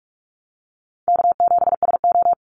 Morse code, heard as a keyed tone, about 44 wpm, 700 Hz. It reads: K7SO
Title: NA SOTA Activator Callsigns 45wpm